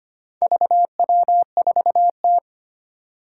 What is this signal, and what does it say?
Morse code, heard as a keyed tone, about 25 wpm, 700 Hz. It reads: VW4T
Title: Four Challenging Letter-Number Combinations 25wpm